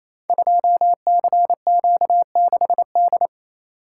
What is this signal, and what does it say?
Morse code, heard as a keyed tone, about 27 wpm, 700 Hz. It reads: 2CQ6D
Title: Code Groups - 5 Letter:Numbers - 28wpm